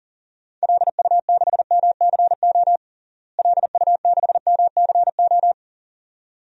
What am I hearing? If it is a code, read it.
Morse code, heard as a keyed tone, about 40 wpm, 700 Hz. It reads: LU6MCO LU6MCO
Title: DXCC-100 Call Signs - 6 Characters - Repeated 2x - 40wpm